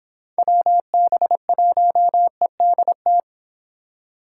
Morse code, heard as a keyed tone, about 26 wpm, 700 Hz. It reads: WB1EDT